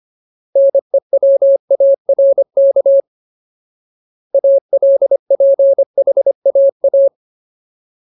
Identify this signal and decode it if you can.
Morse code, heard as a keyed tone, about 25 wpm, 550 Hz. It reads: NEWARK ALPHAA
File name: Ultimate Word List / 25WPM Ultimate